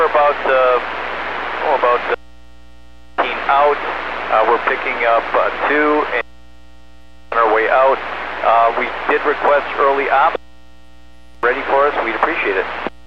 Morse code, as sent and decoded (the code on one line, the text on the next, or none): none